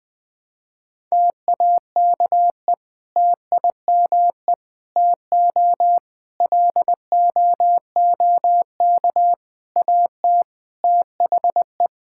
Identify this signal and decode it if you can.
Morse code, heard as a keyed tone, about 20 wpm, 700 Hz. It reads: TAKE TIME TO LOOK AT THE
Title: Sentences with 3-Word Phrases - Spoken Beginning - 20wpm